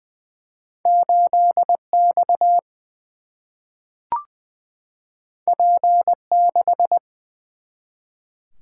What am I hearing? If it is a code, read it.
Morse code, heard as a keyed tone, about 21 wpm, 700 Hz. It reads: 8 X P 6
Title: Single-Letter Single-Number 20wpm